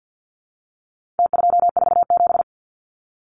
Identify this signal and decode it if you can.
Morse code, heard as a keyed tone, about 50 wpm, 700 Hz. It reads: T247
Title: Numbers - Contest Style - 4-Digit 50wpm